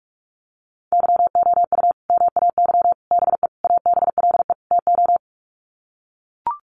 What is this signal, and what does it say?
Morse code, heard as a keyed tone, about 44 wpm, 700 Hz. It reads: YOU MAY BE ABLE TO
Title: Sets of 5 Words Encore 45wpm